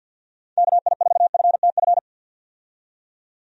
Morse code, heard as a keyed tone, about 52 wpm, 700 Hz. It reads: KI4FTF
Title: USA Call Signs - 6 Characters - 50wpm